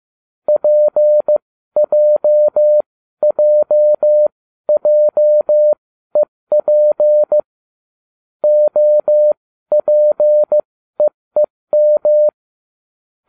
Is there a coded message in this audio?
Yes: Morse code from 0.5 to 12.3 s, heard as a keyed tone, about 15 wpm, spaced as for 12 wpm, 600 Hz. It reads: PJJJEP OPEEM